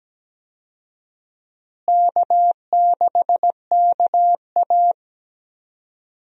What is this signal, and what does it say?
Morse code, heard as a keyed tone, about 17 wpm, 700 Hz. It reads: K6KA
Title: USA Call Signs - 4 Characters - Repeated 2x - 17wpm